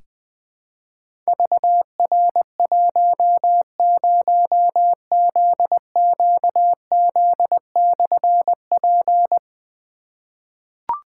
Morse code, heard as a keyed tone, about 20 wpm, 700 Hz. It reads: VR10ZQZ/P
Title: Difficult Call Signs 20wpm